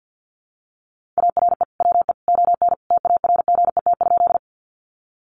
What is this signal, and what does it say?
Morse code, heard as a keyed tone, about 50 wpm, 700 Hz. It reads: ARE WE ON TARGET?
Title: Sentences from Top 1200 Words 50wpm